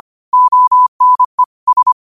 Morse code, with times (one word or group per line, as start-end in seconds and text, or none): 0.3-1.4 ONE
1.7-1.9 S